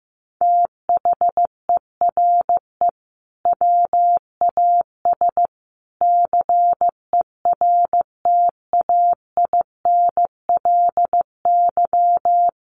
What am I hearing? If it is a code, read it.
Morse code, heard as a keyed tone, about 14 wpm, 700 Hz. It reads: THERE WAS CERTAINLY
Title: Sentences from Top 1400 Words 15wpm